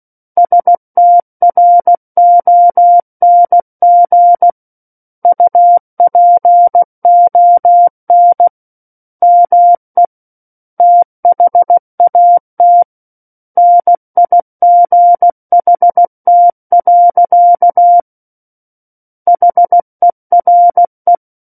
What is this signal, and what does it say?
Morse code, heard as a keyed tone, about 16 wpm, 700 Hz. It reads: STRONG UPON ME THAT NIGHT. HERE
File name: Book One / 07-Chapter 7 - 16 WPM